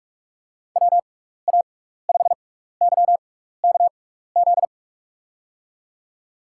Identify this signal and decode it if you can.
Morse code, heard as a keyed tone, about 46 wpm, spaced as for 15 wpm, 700 Hz. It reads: WA5YKZ
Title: USA Call Signs - 6 Characters - 6x Character Spacing - 45wpm